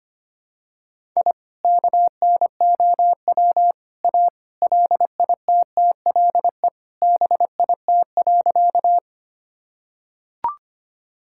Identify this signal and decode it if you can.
Morse code, heard as a keyed tone, about 25 wpm, 700 Hz. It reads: I KNOW A LITTLE BIT.
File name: practice / 25wpm / Sentences with 3-Word Phrases - 25wpm